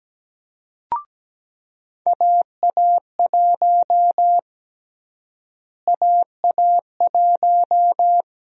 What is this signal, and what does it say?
Morse code, heard as a keyed tone, about 17 wpm, 700 Hz. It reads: AA1 AA1
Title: USA Call Sign Prefixes - Repeated 2x - 17wpm